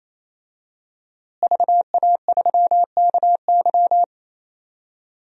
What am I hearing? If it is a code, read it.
Morse code, heard as a keyed tone, about 28 wpm, 700 Hz. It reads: VA3KY
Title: DXCC-100 Call Signs - 5 Characters - Repeated 3x - 28wpm